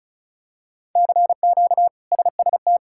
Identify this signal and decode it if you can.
Morse code, heard as a keyed tone, about 36 wpm, 700 Hz. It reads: CQ SST
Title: K1USN SST QSO Practice - 35wpm